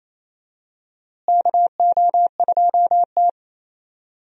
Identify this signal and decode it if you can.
Morse code, heard as a keyed tone, about 28 wpm, 700 Hz. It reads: KO2T
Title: USA Call Signs - 4 Characters - Repeated 3x - 28wpm